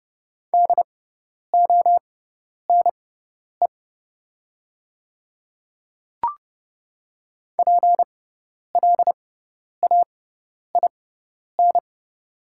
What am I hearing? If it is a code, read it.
Morse code, heard as a keyed tone, about 30 wpm, spaced as for 10 wpm, 700 Hz. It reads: DONE PLAIN
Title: 6x Character Spacing - Top 500 Words - 30wpm